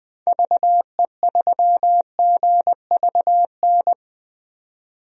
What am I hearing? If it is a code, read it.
Morse code, heard as a keyed tone, about 20 wpm, 700 Hz. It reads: VE3GVN
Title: DXCC-100 Call Signs - 6 Characters - 20wpm